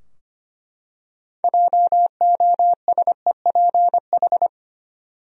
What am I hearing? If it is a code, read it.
Morse code, heard as a keyed tone, about 25 wpm, 700 Hz. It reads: JOSEPH
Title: USA Common Names - 25wpm